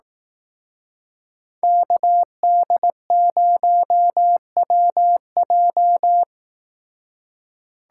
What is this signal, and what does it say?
Morse code, heard as a keyed tone, about 17 wpm, 700 Hz. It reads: KD0WJ
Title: Speed Racing - USA Call Signs 15wpm